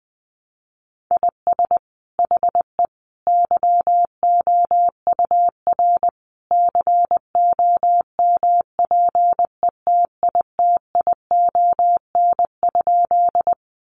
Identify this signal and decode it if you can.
Morse code, heard as a keyed tone, about 19 wpm, 700 Hz. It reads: IS HE YOUR COMPETITION?
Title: Sentences from Top 1000 Words 20wpm